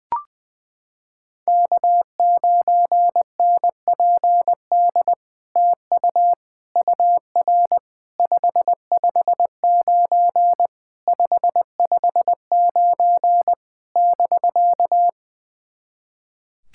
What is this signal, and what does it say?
Morse code, heard as a keyed tone, about 20 wpm, 700 Hz. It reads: K9NPD TU UR 559 559 <BK>
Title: SOTA QSO Practice 20wpm